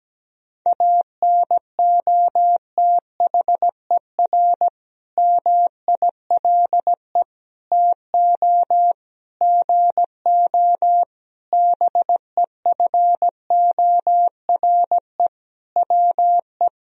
Morse code, heard as a keyed tone, about 17 wpm, 700 Hz. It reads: ANOTHER MILE TO GO BEFORE WE
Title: Sentences from Top 1000 Words - Review Part 1 - 17wpm